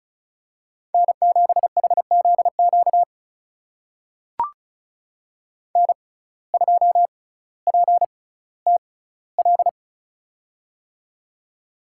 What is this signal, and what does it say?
Morse code, heard as a keyed tone, about 36 wpm, spaced as for 18 wpm, 700 Hz. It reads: N7HZQ N2PTL
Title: USA Call Signs - 5 Characters - 6x Character Spacing - 35wpm